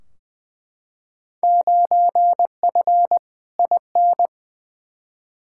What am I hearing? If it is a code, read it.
Morse code, heard as a keyed tone, about 20 wpm, 700 Hz. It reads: 9F IN
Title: Field Day Exchanges Encore - 20wpm